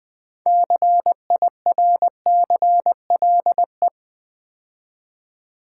Morse code, heard as a keyed tone, about 20 wpm, 700 Hz. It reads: CIRCLE
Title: 6 Letter Words - First 3 Letters Given - 20wpm